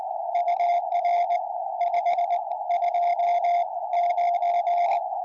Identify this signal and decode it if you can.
Morse code, heard as a keyed tone, about 20 wpm, spaced as for 17 wpm, 700 Hz, with radio noise. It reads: UR 529
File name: tropo / g4rqi 70cm